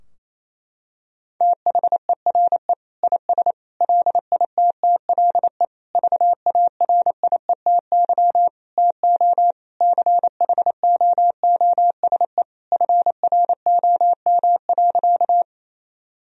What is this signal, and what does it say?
Morse code, heard as a keyed tone, about 28 wpm, 700 Hz. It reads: THERE IS LITTLE VARIETY TO CHOOSE FROM.
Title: Sentences from Top 1000 Words - Review Part 4 - 28wpm